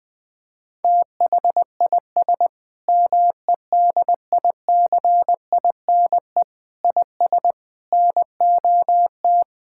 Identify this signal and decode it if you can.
Morse code, heard as a keyed tone, about 20 wpm, 700 Hz. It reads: THIS MEDICINE IS NOT
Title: Sentences from Top 1000 Words - Review Part 6 - 20wpm